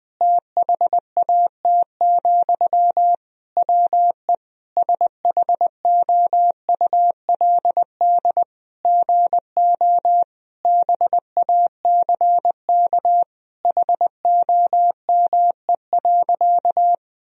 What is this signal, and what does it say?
Morse code, heard as a keyed tone, about 21 wpm, 700 Hz. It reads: THAT, WE SHOULD GO BACK HOME.
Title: Sentences from Top 1000 Words - Review Part 4 - 20wpm